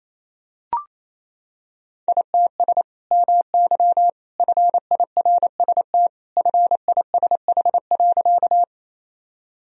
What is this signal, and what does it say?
Morse code, heard as a keyed tone, about 28 wpm, 700 Hz. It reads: ITS MY FIRST FISH.
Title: Sentences from Top 1000 Words - Review Part 1 - 28wpm